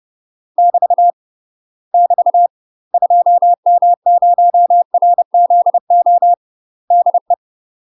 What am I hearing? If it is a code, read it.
Morse code, heard as a keyed tone, about 30 wpm, 700 Hz. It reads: <BT> <BT> 2M0RZO DE
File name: UK MASTER QSOs TRAINING / 30WPM / UK MASTER QSOs 0044